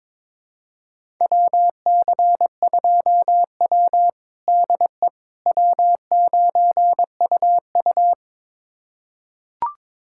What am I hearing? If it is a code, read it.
Morse code, heard as a keyed tone, about 22 wpm, 700 Hz. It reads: WC2W DE W9UU
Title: USA Callsign DE Callsign - 4 Characters - 22wpm